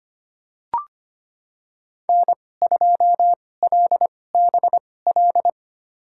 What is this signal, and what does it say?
Morse code, heard as a keyed tone, about 25 wpm, spaced as for 18 wpm, 700 Hz. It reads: N2LBL
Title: USA Call Signs - 5 Characters - 2x Character Spacing - 25wpm